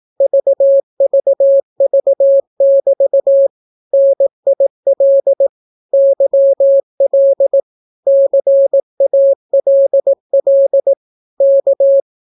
Morse code, heard as a keyed tone, about 18 wpm, 550 Hz. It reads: VVV<BT> NIL YL CALL K
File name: naukapodstaw / skroty / skroty